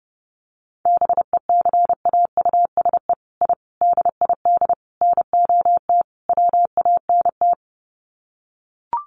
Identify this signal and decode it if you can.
Morse code, heard as a keyed tone, about 30 wpm, 700 Hz. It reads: BECAUSE I DID NOT WANT
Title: Sets of 5 Words Encore2 30wpm